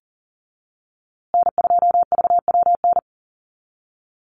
Morse code, heard as a keyed tone, about 40 wpm, 700 Hz. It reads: N2VWN
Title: USA Call Signs Repeated 3x 40wpm